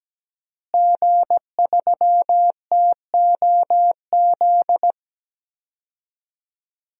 Morse code, heard as a keyed tone, about 17 wpm, 700 Hz. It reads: G3TOZ
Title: DXCC-100 Call Signs - 5 Characters - Repeated 3x - 17wpm